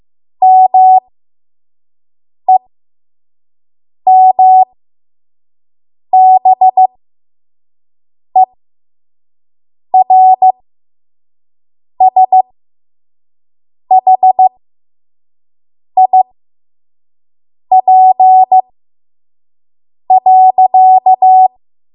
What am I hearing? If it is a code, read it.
Morse code, heard as a keyed tone, about 15 wpm, spaced as for 5 wpm, 750 Hz. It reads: MEMBERSHIP.
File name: Morse / 05WPM